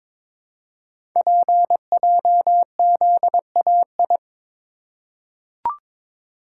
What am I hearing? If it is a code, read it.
Morse code, heard as a keyed tone, about 22 wpm, 700 Hz. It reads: PJZAI
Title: Code Groups - 5 Letters - 22wpm